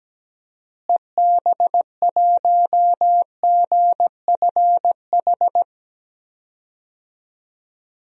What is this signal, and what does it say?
Morse code, heard as a keyed tone, about 17 wpm, 700 Hz. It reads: EB1GFH
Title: DXCC-100 Call Signs - 6 Characters - Repeated 3x - 17wpm